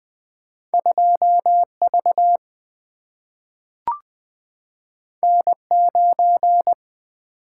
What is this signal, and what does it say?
Morse code, heard as a keyed tone, about 20 wpm, 700 Hz. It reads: 2V N9
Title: Single-Letter Single-Number 20wpm